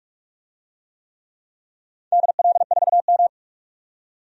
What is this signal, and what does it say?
Morse code, heard as a keyed tone, about 44 wpm, 700 Hz. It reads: DL4M
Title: DXCC-100 Call Signs - 4 Characters - 45wpm